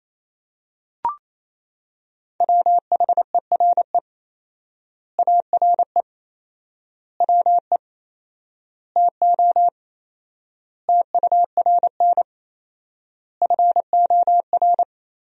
Morse code, heard as a keyed tone, about 28 wpm, 700 Hz. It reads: WHERE ARE WE TO TURN FOR
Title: 4x Word-Spacing - Sentences from Top 200 Words - 28wpm